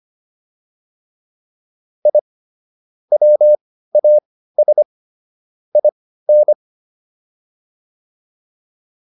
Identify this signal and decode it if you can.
Morse code, heard as a keyed tone, about 25 wpm, spaced as for 15 wpm, 600 Hz. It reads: I WAS IN